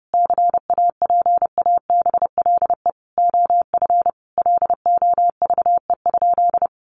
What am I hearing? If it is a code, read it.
Morse code, heard as a keyed tone, about 30 wpm, 700 Hz. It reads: CAPABLE OF LOVE?
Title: Sentences from Top 1200 Words 30wpm